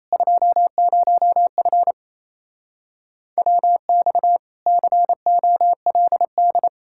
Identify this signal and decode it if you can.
Morse code, heard as a keyed tone, about 37 wpm, 700 Hz. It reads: 20F WX COLD